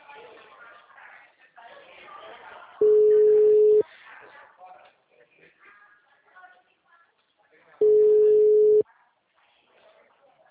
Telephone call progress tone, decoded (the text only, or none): ringback tone